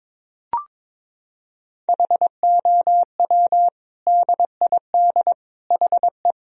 Morse code, heard as a keyed tone, about 21 wpm, 700 Hz. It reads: HOW DID HE
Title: Sentences from Top 1000 Words - Review Part 2 - 22wpm